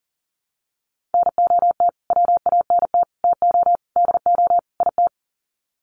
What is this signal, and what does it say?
Morse code, heard as a keyed tone, about 40 wpm, 700 Hz. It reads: NOT WANT TO DO IT
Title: Sets of 5 Words Encore 40wpm